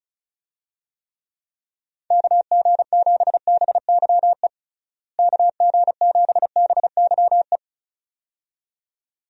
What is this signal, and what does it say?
Morse code, heard as a keyed tone, about 36 wpm, 700 Hz. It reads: KG7BYE KG7BYE
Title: USA Call Signs - 6 Characters - Repeated 2x - 35wpm